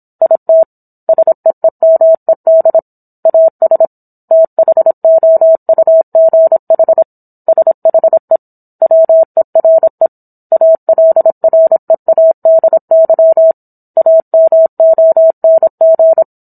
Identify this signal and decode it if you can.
Morse code, heard as a keyed tone, about 27 wpm, 650 Hz. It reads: IT SEEMED AS THOUGH SHE WERE ALREADY AMONG